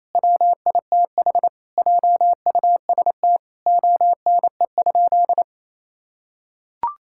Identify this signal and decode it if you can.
Morse code, heard as a keyed tone, about 27 wpm, 700 Hz. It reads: WITH JUST ONE?
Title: Sentences from Top 1000 Words - Review Part 6 - 28wpm